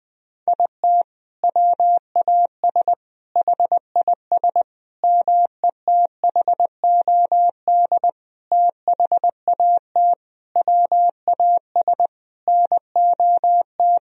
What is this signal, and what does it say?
Morse code, heard as a keyed tone, about 20 wpm, 700 Hz. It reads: IT WAS HIS METHOD THAT WAS NOT
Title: Sentences from Top 1000 Words - Review Part 3 - 20wpm